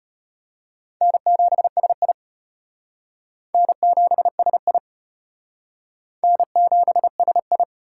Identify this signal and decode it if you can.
Morse code, heard as a keyed tone, about 32 wpm, 700 Hz. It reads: N7SI N7SI N7SI